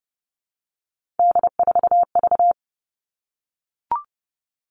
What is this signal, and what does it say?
Morse code, heard as a keyed tone, about 30 wpm, 700 Hz. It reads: D4V
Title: All Cut Numbers - 3-Digit 30wpm